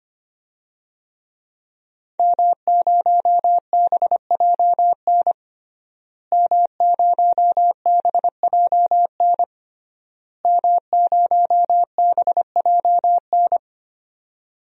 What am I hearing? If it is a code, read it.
Morse code, heard as a keyed tone, about 25 wpm, 700 Hz. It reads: M0BJN M0BJN M0BJN
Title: DXCC-100 Call Signs - 5 Characters - Repeated 3x - 25wpm